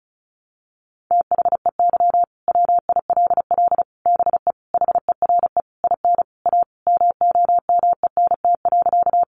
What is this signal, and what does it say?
Morse code, heard as a keyed tone, about 34 wpm, 700 Hz. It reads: THEY WILL BE HERE IN A MOMENT.
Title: Sentences from Top 1000 Words 35wpm